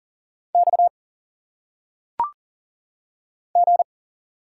Morse code, heard as a keyed tone, about 40 wpm, 700 Hz. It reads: X G
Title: Single Letters - Phonetic Pronunciation 40wpm